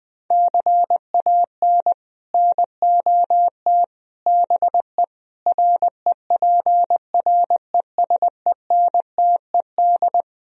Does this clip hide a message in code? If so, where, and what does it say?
Yes: Morse code from 0.3 to 10.2 s, heard as a keyed tone, about 20 wpm, 700 Hz. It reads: CAN NOT BE REPRESENTED